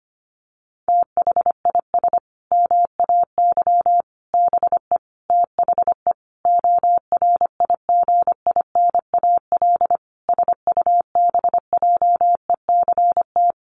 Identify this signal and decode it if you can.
Morse code, heard as a keyed tone, about 25 wpm, 700 Hz. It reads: THIS MAY BE THE ORIGINAL SUBJECT
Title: Sentences from Top 900 Words 25wpm